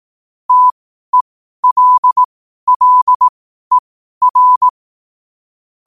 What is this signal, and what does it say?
Morse code, heard as a keyed tone, about 18 wpm, spaced as for 12 wpm, 1000 Hz. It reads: TELLER
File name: qotdfiles / 12wpm